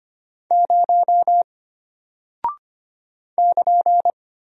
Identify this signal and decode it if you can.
Morse code, heard as a keyed tone, about 25 wpm, 700 Hz. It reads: 0 <KN>
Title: Single Character (Letter:Number:Punctuation:Prosign) Alt 25wpm